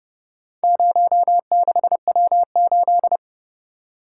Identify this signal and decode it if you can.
Morse code, heard as a keyed tone, about 28 wpm, 700 Hz. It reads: 06W8